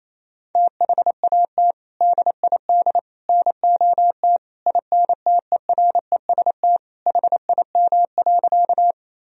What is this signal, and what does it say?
Morse code, heard as a keyed tone, about 28 wpm, 700 Hz. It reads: THAT DID NOT INTEREST HIM.
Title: Sentences from Top 1000 Words - Review Part 2 - 28wpm